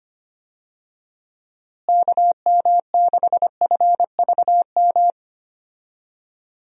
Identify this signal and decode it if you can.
Morse code, heard as a keyed tone, about 25 wpm, 700 Hz. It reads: KM6FVM
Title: USA Call Signs - 6 Characters - Repeated 2x - 25wpm